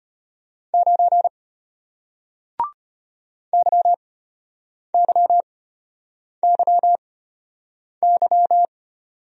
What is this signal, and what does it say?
Morse code, heard as a keyed tone, about 34 wpm, spaced as for 8 wpm, 700 Hz. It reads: 9 YYYY